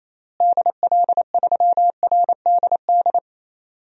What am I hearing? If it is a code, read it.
Morse code, heard as a keyed tone, about 27 wpm, 700 Hz. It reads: DL3RDD
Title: DXCC-100 Call Signs - 6 Characters - Repeated 3x - 28wpm